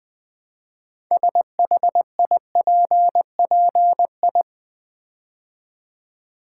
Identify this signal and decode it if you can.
Morse code, heard as a keyed tone, about 20 wpm, 700 Hz. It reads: SHIPPI
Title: Build A Word - 3 Letters At A Time - 8-Letter Words - 20wpm